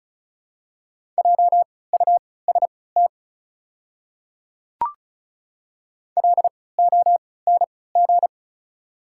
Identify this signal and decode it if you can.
Morse code, heard as a keyed tone, about 36 wpm, spaced as for 20 wpm, 700 Hz. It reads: JUST LONG
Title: 3x Character Spacing - Top 200 Words - 35wpm